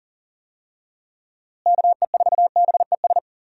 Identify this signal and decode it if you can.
Morse code, heard as a keyed tone, about 40 wpm, 700 Hz. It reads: KE4BES
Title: USA Call Signs - 6 Characters - Repeated 2x - 40wpm